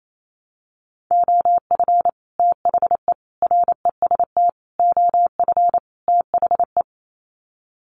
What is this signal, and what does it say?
Morse code, heard as a keyed tone, about 28 wpm, 700 Hz. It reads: OF THE REST OF THE
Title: Sets of 5 Words 28wpm